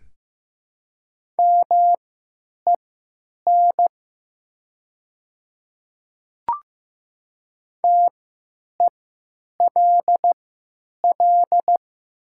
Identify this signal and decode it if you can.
Morse code, heard as a keyed tone, about 15 wpm, spaced as for 9 wpm, 700 Hz. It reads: MEN TELL
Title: 3x Character Spacing - Top 300 Words - 15wpm